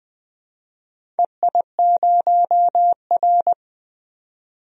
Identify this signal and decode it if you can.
Morse code, heard as a keyed tone, about 20 wpm, 700 Hz. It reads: EI0R
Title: DXCC-100 Call Signs - 4 Characters - 20wpm